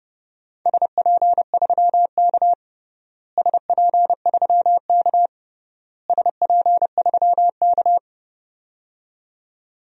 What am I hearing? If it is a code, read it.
Morse code, heard as a keyed tone, about 29 wpm, 700 Hz. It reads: SP3K SP3K SP3K
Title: DXCC-100 Call Signs - 4 Characters - Repeated 3x - 30wpm